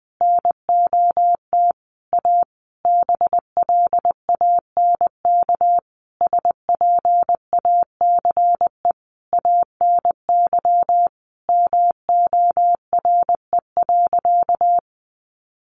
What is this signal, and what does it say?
Morse code, heard as a keyed tone, about 20 wpm, 700 Hz. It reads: NOT A BLANK SPACE ANY MORE.
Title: Sentences from Top 1500 Words 20wpm